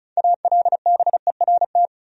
Morse code, heard as a keyed tone, about 36 wpm, 700 Hz. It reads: ALBERT
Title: USA Common Names - 35wpm